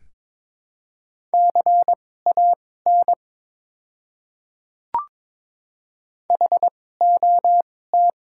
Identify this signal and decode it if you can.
Morse code, heard as a keyed tone, about 22 wpm, spaced as for 16 wpm, 700 Hz. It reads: CAN HOT
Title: 2x Character Spacing - Top 100 Words - 22wpm